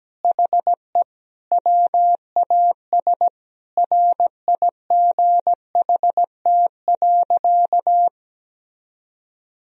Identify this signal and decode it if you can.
Morse code, heard as a keyed tone, about 17 wpm, 700 Hz. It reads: HE WAS RIGHT.
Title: Sentences from Top 1000 Words - Review Part 4 - 17wpm